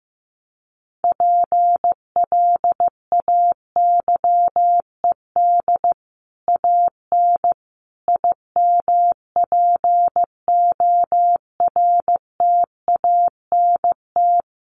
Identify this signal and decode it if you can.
Morse code, heard as a keyed tone, about 15 wpm, 700 Hz. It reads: PLAYED AN IMPORTANT
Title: Sets of 5 Words Encore2 15wpm